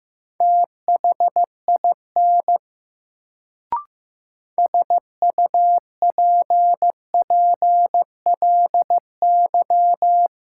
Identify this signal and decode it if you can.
Morse code, heard as a keyed tone, about 15 wpm, 700 Hz. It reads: THIN SUPPLY
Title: Sentences from Top 1200 Words - New Word Drill Alt - 15wpm